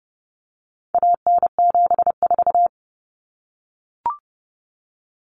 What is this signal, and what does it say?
Morse code, heard as a keyed tone, about 30 wpm, 700 Hz. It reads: AN74
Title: Common Cut Numbers - 4-Digit 30wpm